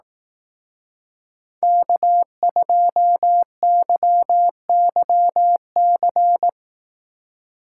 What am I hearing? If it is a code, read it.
Morse code, heard as a keyed tone, about 18 wpm, 700 Hz. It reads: K2YYC
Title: Speed Racing - USA 1x3 Call Signs 15wpm